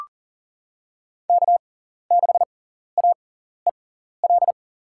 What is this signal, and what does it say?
Morse code, heard as a keyed tone, about 40 wpm, spaced as for 14 wpm, 700 Hz. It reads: K6AEL
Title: USA Call Signs - 5 Characters - 6x Character Spacing - 40wpm